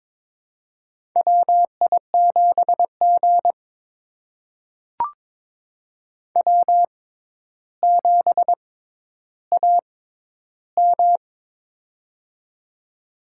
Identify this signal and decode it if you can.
Morse code, heard as a keyed tone, about 22 wpm, spaced as for 11 wpm, 700 Hz. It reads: WI7G W7AM
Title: USA Call Signs - 4 Characters - 6x Character Spacing - 22wpm